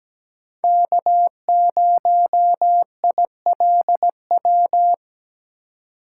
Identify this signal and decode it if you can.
Morse code, heard as a keyed tone, about 20 wpm, 700 Hz. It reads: K0ILW
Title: Speed Racing - USA Call Signs 17wpm